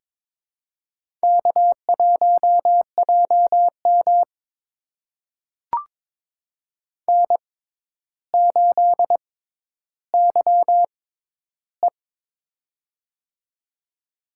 Morse code, heard as a keyed tone, about 22 wpm, spaced as for 11 wpm, 700 Hz. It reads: K1JM N8YE
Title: USA Call Signs - 4 Characters - 6x Character Spacing - 22wpm